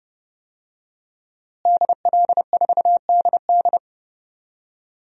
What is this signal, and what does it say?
Morse code, heard as a keyed tone, about 30 wpm, 700 Hz. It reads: DL4DD
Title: DXCC-100 Call Signs - 5 Characters - Repeated 2x - 30wpm